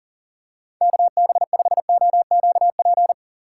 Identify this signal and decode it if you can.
Morse code, heard as a keyed tone, about 40 wpm, 700 Hz. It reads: KB5OQP
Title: USA Call Signs - 6 Characters - 6x Character Spacing - 40wpm